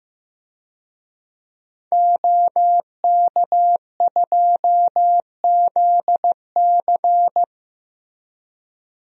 Morse code, heard as a keyed tone, about 15 wpm, 700 Hz. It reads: OK2ZC
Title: DXCC-100 Call Signs - 5 Characters - Repeated 2x - 15wpm